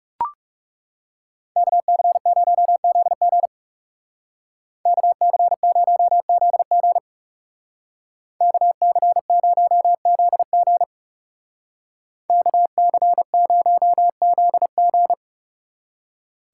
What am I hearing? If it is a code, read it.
Morse code, heard as a keyed tone, about 35 wpm, 700 Hz. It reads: KC0ZG KC0ZG KC0ZG KC0ZG
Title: Speed Racing - USA 2x2 Call Signs 30wpm